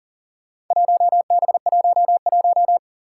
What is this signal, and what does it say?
Morse code, heard as a keyed tone, about 39 wpm, 700 Hz. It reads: 1B11
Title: Four Challenging Letter-Number Combinations 40wpm